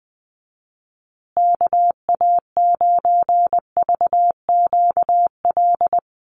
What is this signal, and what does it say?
Morse code, heard as a keyed tone, about 20 wpm, 700 Hz. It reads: KA9VQL
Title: USA Call Signs Repeated 3x 20wpm